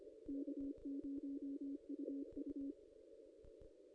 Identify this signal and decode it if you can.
Morse code, heard as a keyed tone, about 25 wpm, 280 Hz. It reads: K0UU
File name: contests / ss-cw-k5tr-op-n5rz -1700